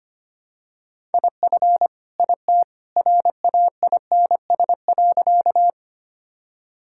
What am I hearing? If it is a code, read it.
Morse code, heard as a keyed tone, about 25 wpm, 700 Hz. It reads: IF IT RAINS.
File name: practice / 25wpm / Sentences with 2-Word Phrases - 25wpm